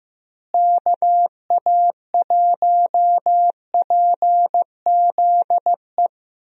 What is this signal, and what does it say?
Morse code, heard as a keyed tone, about 15 wpm, 700 Hz. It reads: KA1PZE
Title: USA Call Signs - 6 Characters - Repeated 3x - 15wpm